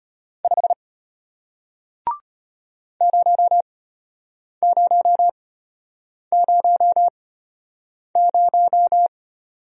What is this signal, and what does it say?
Morse code, heard as a keyed tone, about 33 wpm, spaced as for 8 wpm, 700 Hz. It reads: 5 0000